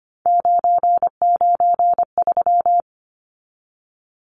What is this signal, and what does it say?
Morse code, heard as a keyed tone, about 25 wpm, 700 Hz. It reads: 993
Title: Numbers - 3-Digit 25wpm